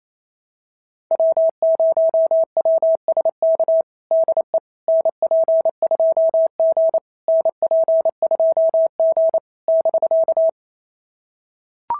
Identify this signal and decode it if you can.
Morse code, heard as a keyed tone, about 25 wpm, 650 Hz. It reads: W0WSK DE NP2G NP2G <BK>
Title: SOTA QSO Practice 28wpm